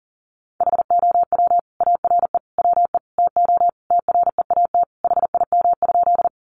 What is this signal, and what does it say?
Morse code, heard as a keyed tone, about 40 wpm, 700 Hz. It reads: HOW ARE WE TO TREAT HIM?
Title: Sentences from Top 1300 Words 40wpm